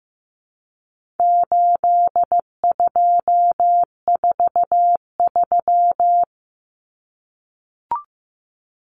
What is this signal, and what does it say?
Morse code, heard as a keyed tone, about 15 wpm, 700 Hz. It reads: 8243
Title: Common Cut Numbers - 4-Digit 15wpm